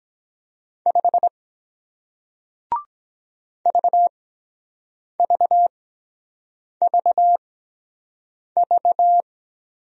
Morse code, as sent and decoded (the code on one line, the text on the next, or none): ..... / ...- ...- ...- ...-
5 VVVV